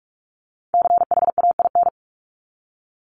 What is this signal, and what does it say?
Morse code, heard as a keyed tone, about 44 wpm, 700 Hz. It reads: CHAIN
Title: Top 1251-1500 Words 45wpm